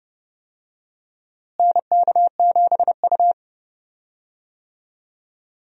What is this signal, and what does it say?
Morse code, heard as a keyed tone, about 30 wpm, 700 Hz. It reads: NK7U
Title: USA Call Signs - 4 Characters - Repeated 3x - 30wpm